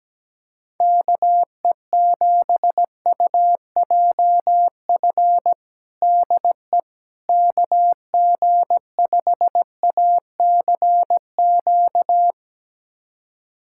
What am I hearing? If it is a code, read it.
Morse code, heard as a keyed tone, about 17 wpm, 700 Hz. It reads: KE7UJF DE KG5ACQ
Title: USA Callsign DE Callsign - 6 Characters - 17wpm